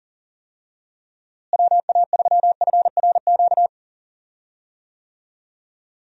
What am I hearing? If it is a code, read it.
Morse code, heard as a keyed tone, about 40 wpm, 700 Hz. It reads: WA3FRQ
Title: USA Call Signs - 6 Characters - Repeated 2x - 40wpm